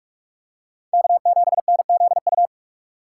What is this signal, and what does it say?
Morse code, heard as a keyed tone, about 41 wpm, 700 Hz. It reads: K7NZU